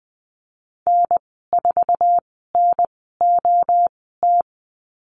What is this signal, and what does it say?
Morse code, heard as a keyed tone, about 20 wpm, spaced as for 15 wpm, 700 Hz. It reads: N4NOT
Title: 048 Lesson 9g - Number 4 - Call Signs 20wpm